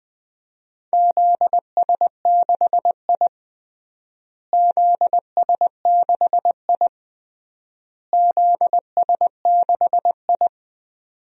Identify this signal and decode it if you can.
Morse code, heard as a keyed tone, about 20 wpm, 700 Hz. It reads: ZS6I ZS6I ZS6I
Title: DXCC-100 Call Signs - 4 Characters - Repeated 3x - 20wpm